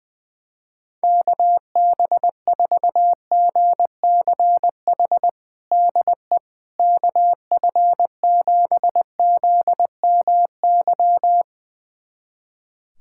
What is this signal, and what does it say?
Morse code, heard as a keyed tone, about 21 wpm, 700 Hz. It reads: KB4GCH DE KF7ZMY
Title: USA Callsign DE Callsign 20wpm